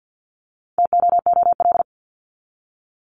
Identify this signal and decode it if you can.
Morse code, heard as a keyed tone, about 50 wpm, 700 Hz. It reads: TOOL
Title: Top 751-1000 Words 50wpm